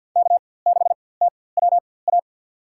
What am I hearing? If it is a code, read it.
Morse code, heard as a keyed tone, about 48 wpm, spaced as for 23 wpm, 700 Hz. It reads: K6TWA
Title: USA Call Signs - 5 Characters - 4x Character Spacing - 50wpm